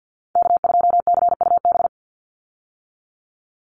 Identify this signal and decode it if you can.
Morse code, heard as a keyed tone, about 50 wpm, 700 Hz. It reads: K2CUB
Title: Call Signs - 50wpm